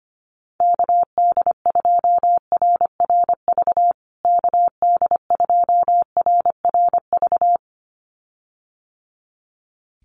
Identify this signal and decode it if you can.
Morse code, heard as a keyed tone, about 25 wpm, 700 Hz. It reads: KD2RRV KD2RRV
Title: USA Call Signs Repeated 2x 25wpm